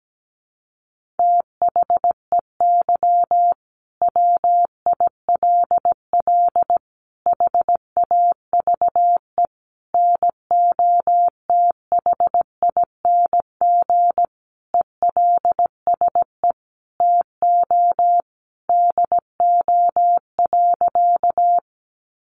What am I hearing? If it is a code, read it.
Morse code, heard as a keyed tone, about 17 wpm, 700 Hz. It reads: THEY WILL HAVE NOTHING ELSE TO DO.
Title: Sentences from Top 800 Words 17wpm